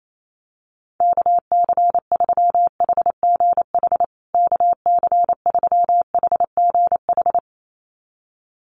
Morse code, heard as a keyed tone, about 28 wpm, 700 Hz. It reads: KC3HGH KC3HGH
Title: USA Call Signs Repeated 2x 28wpm